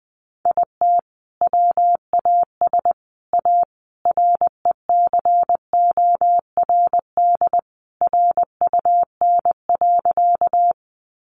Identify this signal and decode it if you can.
Morse code, heard as a keyed tone, about 20 wpm, 700 Hz. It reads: IT WAS A RECORD RUN.
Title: Sentences from Top 500 Words 20wpm